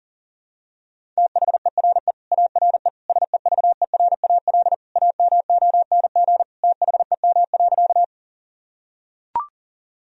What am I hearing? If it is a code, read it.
Morse code, heard as a keyed tone, about 40 wpm, 700 Hz. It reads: THERE ARE SEVERAL AMONG THEM.
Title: Sentences from Top 1000 Words - Review Part 2 - 40wpm